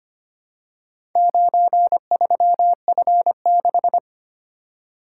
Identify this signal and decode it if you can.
Morse code, heard as a keyed tone, about 25 wpm, 700 Hz. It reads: 93F6
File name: practice / 25wpm / Four Letter-Number Combinations - 25wpm